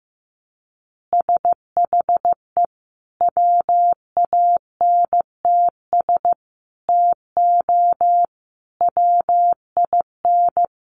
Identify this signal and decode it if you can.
Morse code, heard as a keyed tone, about 15 wpm, 700 Hz. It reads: SHE WANTS TO WIN
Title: Additional Easy Phrases 15wpm